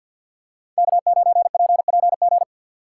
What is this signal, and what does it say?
Morse code, heard as a keyed tone, about 53 wpm, 700 Hz. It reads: K9PPG
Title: USA Call Signs - 5 Characters - 2x Character Spacing - 50wpm